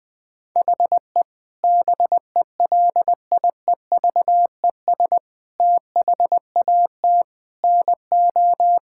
Morse code, heard as a keyed tone, about 20 wpm, 700 Hz. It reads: HE BELIEVES THAT NO
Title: Sentences with 9-Word Phrases - 20wpm